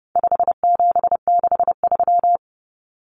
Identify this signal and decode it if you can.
Morse code, heard as a keyed tone, about 30 wpm, 700 Hz. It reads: 5763
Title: Common Cut Numbers - 4-Digit 30wpm